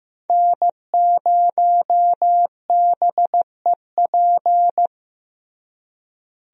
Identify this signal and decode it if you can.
Morse code, heard as a keyed tone, about 15 wpm, 700 Hz. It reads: N0BEP
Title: USA Call Signs - 5 Characters - Repeated 3x - 15wpm